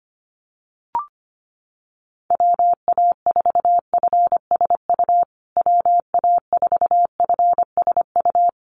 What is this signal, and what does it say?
Morse code, heard as a keyed tone, about 25 wpm, 700 Hz. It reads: WA4FSU WA4FSU
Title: USA Call Signs Repeated 2x 25wpm